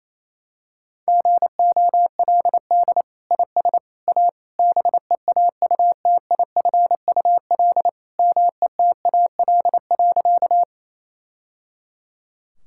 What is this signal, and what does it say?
Morse code, heard as a keyed tone, about 28 wpm, 700 Hz. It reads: GOLD IS A BEAUTIFUL METAL.
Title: Sentences from Top 1000 Words - Review Part 6 - 28wpm